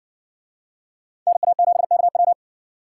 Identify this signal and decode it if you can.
Morse code, heard as a keyed tone, about 57 wpm, 700 Hz. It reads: NA7KW